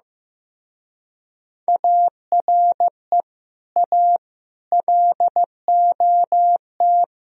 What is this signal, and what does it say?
Morse code, heard as a keyed tone, about 16 wpm, 700 Hz. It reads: ARE A LOT
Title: Speed-Racing - Sets of 4 Words 15wpm